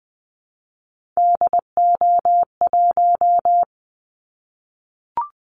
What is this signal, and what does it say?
Morse code, heard as a keyed tone, about 19 wpm, 700 Hz. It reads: DO1
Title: Callsign Prefixes for Top 25 DXCC - 20wpm